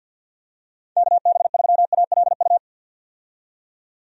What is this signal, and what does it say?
Morse code, heard as a keyed tone, about 47 wpm, 700 Hz. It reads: KB3ALU
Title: USA Call Signs - 6 Characters - 50wpm